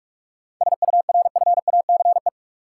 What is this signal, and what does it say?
Morse code, heard as a keyed tone, about 43 wpm, 700 Hz. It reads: SURFACE
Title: Sentences from Top 600 Words - New Word Drill - 45wpm